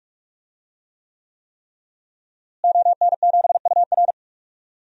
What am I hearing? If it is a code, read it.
Morse code, heard as a keyed tone, about 46 wpm, 700 Hz. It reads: ON7UR
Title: DXCC-100 Call Signs - 5 Characters - Repeated 3x - 45wpm